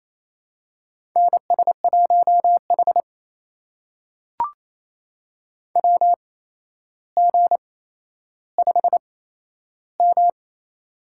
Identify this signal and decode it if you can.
Morse code, heard as a keyed tone, about 28 wpm, spaced as for 12 wpm, 700 Hz. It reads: NS1H WG5M
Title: USA Call Signs - 4 Characters - 8x Character Spacing - 28wpm